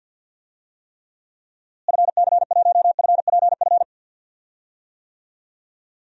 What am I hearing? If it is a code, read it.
Morse code, heard as a keyed tone, about 52 wpm, 700 Hz. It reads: FC1FPF
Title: DXCC-100 Call Signs - 6 Characters - Repeated 3x - 50wpm